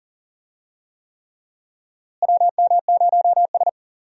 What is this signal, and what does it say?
Morse code, heard as a keyed tone, about 40 wpm, 700 Hz. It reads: WM0S
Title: USA Call Signs - 4 Characters - Repeated 3x - 40wpm